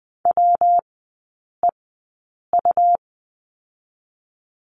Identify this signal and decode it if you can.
Morse code, heard as a keyed tone, about 20 wpm, 700 Hz. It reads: W E U
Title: 114 Lesson 19e - Letter w - Three Characters 20wpm